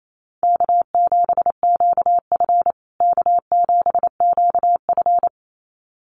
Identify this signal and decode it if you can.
Morse code, heard as a keyed tone, about 28 wpm, 700 Hz. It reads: K7QF K7QF
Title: USA Call Signs Repeated 2x 28wpm